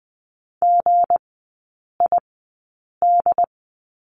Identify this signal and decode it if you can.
Morse code, heard as a keyed tone, about 20 wpm, spaced as for 8 wpm, 700 Hz. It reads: GID
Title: 159 Lesson 26e - Letter g - Three Characters 20wpm